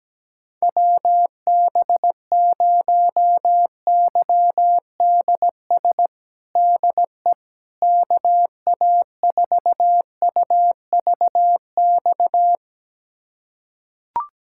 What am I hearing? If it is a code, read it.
Morse code, heard as a keyed tone, about 17 wpm, 700 Hz. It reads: WB0YDS DE KA4UVX
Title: USA Callsign DE Callsign - 6 Characters - 17wpm